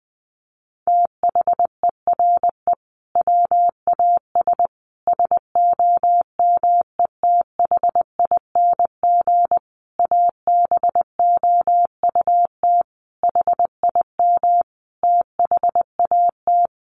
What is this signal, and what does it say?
Morse code, heard as a keyed tone, about 20 wpm, 700 Hz. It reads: THERE WAS SOMETHING ABOUT HIM THAT
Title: Sentences from Top 1500 Words 20wpm